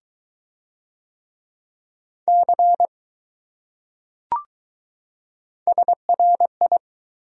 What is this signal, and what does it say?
Morse code, heard as a keyed tone, about 20 wpm, 700 Hz. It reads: C SRI